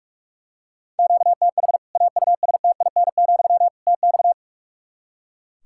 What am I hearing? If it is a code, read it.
Morse code, heard as a keyed tone, about 41 wpm, 700 Hz. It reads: QTH AUSTIN, TX
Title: Practice QSO - Exchange 1 - Beginning - 45wpm